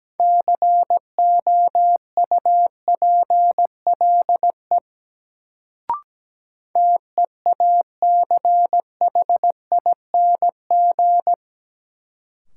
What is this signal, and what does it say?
Morse code, heard as a keyed tone, about 17 wpm, 700 Hz. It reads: COUPLE TEACHING
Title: Sentences from Top 1100 Words - New Word Drill - 17wpm